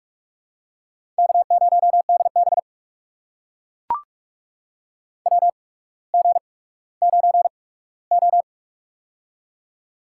Morse code, heard as a keyed tone, about 42 wpm, spaced as for 19 wpm, 690 Hz. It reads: K0DB WG9O
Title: USA Call Signs - 4 Characters - 8x Character Spacing - 45wpm